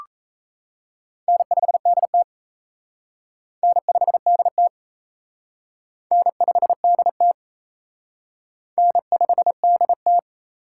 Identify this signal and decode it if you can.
Morse code, heard as a keyed tone, about 34 wpm, 700 Hz. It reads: N5DT N5DT N5DT N5DT
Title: Speed Racing - USA 1x2 Call Signs 28wpm